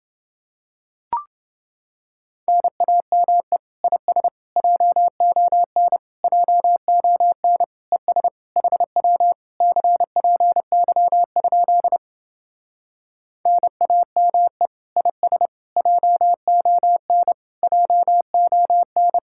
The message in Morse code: NAME IS JON JON ES HW CPY? NAME IS JON JON

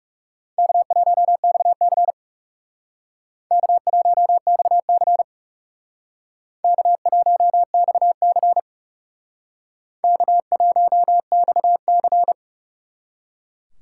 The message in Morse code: K1XC K1XC K1XC K1XC

